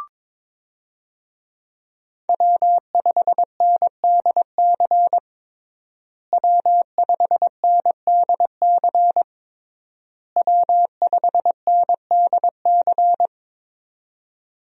Morse code, heard as a keyed tone, about 23 wpm, 700 Hz. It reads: W5NDC W5NDC W5NDC
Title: USA Call Signs - 5 Characters - Repeated 3x - 22wpm